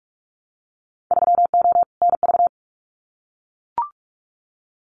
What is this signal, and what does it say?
Morse code, heard as a keyed tone, about 44 wpm, 700 Hz. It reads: 3O NV
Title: Winter Field Day Exchanges Encore 45wpm